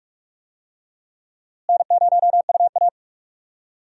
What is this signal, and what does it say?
Morse code, heard as a keyed tone, about 46 wpm, 700 Hz. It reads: N0UA
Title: USA Call Signs - 4 Characters - Repeated 3x - 45wpm